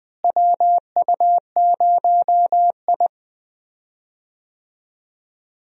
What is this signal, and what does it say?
Morse code, heard as a keyed tone, about 20 wpm, 700 Hz. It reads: WU0I
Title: USA Call Signs - 4 Characters - Repeated 3x - 20wpm